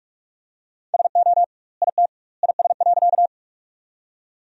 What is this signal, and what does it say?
Morse code, heard as a keyed tone, about 42 wpm, 700 Hz. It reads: SO IT IS.
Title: 2x Word-Spacing - Sentences from Top 100 Words - 45wpm